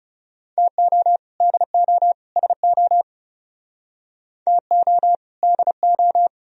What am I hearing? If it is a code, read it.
Morse code, heard as a keyed tone, about 36 wpm, 700 Hz. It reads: TO DO SO TO DO
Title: Speed-Racing - Sets of 3 Words 30wpm